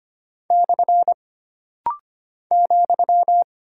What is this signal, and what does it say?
Morse code, heard as a keyed tone, about 25 wpm, 700 Hz. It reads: / ,
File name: Single Punctuation-Prosign - Rapid-Fire - 25wpm